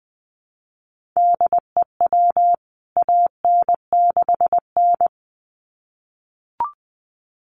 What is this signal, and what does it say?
Morse code, heard as a keyed tone, about 20 wpm, 700 Hz. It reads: DEW AN6N
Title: CWT Exchange 20wpm